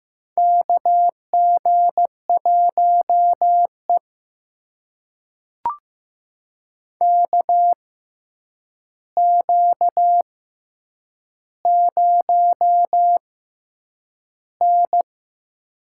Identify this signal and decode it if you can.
Morse code, heard as a keyed tone, about 15 wpm, spaced as for 8 wpm, 700 Hz. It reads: KG1E KQ0N
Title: USA Call Signs - 4 Characters - 6x Character Spacing - 15wpm